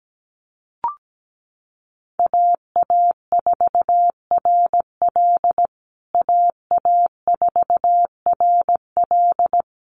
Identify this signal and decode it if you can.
Morse code, heard as a keyed tone, about 17 wpm, 700 Hz. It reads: AA4RL AA4RL
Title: USA Call Signs Repeated 2x 17wpm